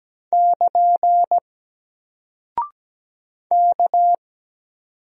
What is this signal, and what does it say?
Morse code, heard as a keyed tone, about 17 wpm, 700 Hz. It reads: <KN> K
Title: Single Character (Letter:Number:Punctuation:Prosign) Alt 17wpm